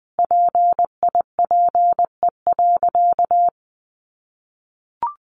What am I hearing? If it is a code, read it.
Morse code, heard as a keyed tone, about 20 wpm, 700 Hz. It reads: PIPE.
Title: Sentences from Top 1500 Words 20wpm